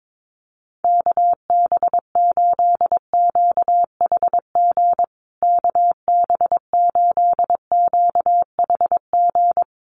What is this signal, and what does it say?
Morse code, heard as a keyed tone, about 22 wpm, 700 Hz. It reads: KB8QHG KB8QHG
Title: USA Call Signs Repeated 2x 22wpm